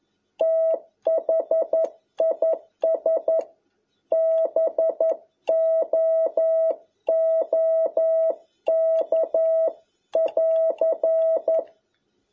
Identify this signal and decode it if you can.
Morse code, heard as a keyed tone, about 11 wpm, 650 Hz. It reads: THIS BOOK <AR>
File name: jh8cbh / 2021cw